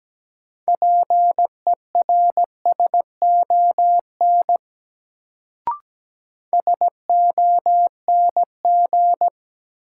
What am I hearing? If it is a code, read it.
Morse code, heard as a keyed tone, about 17 wpm, 700 Hz. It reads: PERSON SONG
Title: Sentences from Top 400 Words - New Word Drill - 17wpm